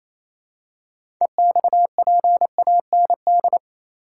Morse code, heard as a keyed tone, about 28 wpm, 700 Hz. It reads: EXPAND
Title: Six Letter Words - 28wpm